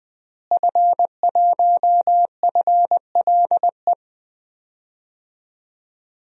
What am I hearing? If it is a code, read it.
Morse code, heard as a keyed tone, about 20 wpm, 700 Hz. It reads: F1FLE
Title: DXCC-100 Call Signs - 5 Characters - Repeated 2x - 20wpm